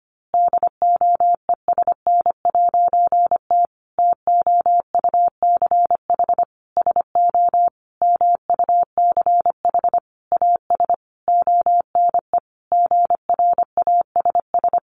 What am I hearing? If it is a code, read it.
Morse code, heard as a keyed tone, about 25 wpm, 700 Hz. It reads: DOESN'T TOUCH SO MUCH AS ONE GRASS
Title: A Sound Of Thunder 25wpm